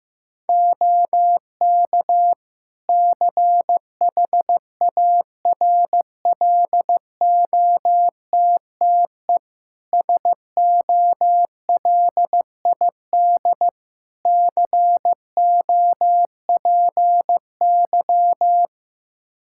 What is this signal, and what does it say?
Morse code, heard as a keyed tone, about 15 wpm, 700 Hz. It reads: OK CHARLOTTE SOLID COPY
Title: Practice QSO - Exchange 3 - More Info - 15wpm